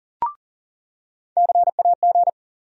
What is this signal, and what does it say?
Morse code, heard as a keyed tone, about 40 wpm, 700 Hz. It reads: CAG
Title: USA Call Sign Postfixes - 3 Letters 40wpm